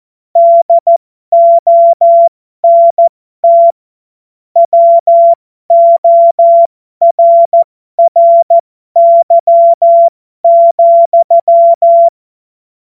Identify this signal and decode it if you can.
Morse code, heard as a keyed tone, about 14 wpm, spaced as for 12 wpm, 680 Hz. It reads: DONT WORRY,
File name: Morsekurs / Witze